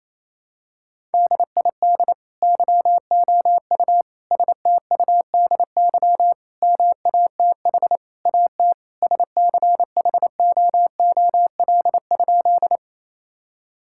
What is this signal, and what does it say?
Morse code, heard as a keyed tone, about 28 wpm, 700 Hz. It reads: DID YOU STUDY MATH AT SCHOOL?
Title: Sentences from Top 1000 Words - Review Part 6 - 28wpm